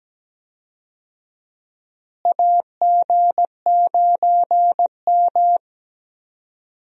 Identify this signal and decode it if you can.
Morse code, heard as a keyed tone, about 17 wpm, 700 Hz. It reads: AG9M
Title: USA Call Signs - 4 Characters - Repeated 2x - 17wpm